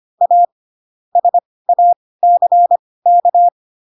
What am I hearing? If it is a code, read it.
Morse code, heard as a keyed tone, about 25 wpm, spaced as for 18 wpm, 700 Hz. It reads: A SACK